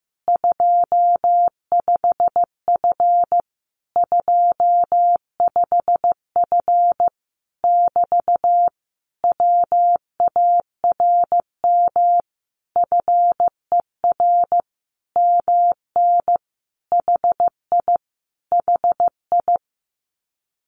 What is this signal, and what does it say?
Morse code, heard as a keyed tone, about 15 wpm, 700 Hz. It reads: 25F 25F <BT> WARM FER MN HI HI
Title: Common QSO Phrases Encore 15wpm